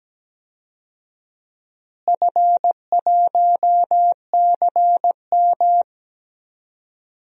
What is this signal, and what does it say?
Morse code, heard as a keyed tone, about 17 wpm, 700 Hz. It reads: F1CM
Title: DXCC-100 Call Signs - 4 Characters - Repeated 2x - 17wpm